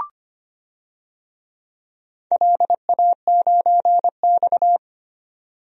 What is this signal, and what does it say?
Morse code, heard as a keyed tone, about 25 wpm, 700 Hz. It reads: LA9X
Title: DXCC-100 Call Signs - 4 Characters - 25wpm